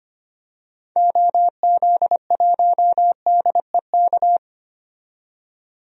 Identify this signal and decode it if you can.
Morse code, heard as a keyed tone, about 25 wpm, 700 Hz. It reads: OZ1DEK
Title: DXCC-100 Call Signs - 6 Characters - 25wpm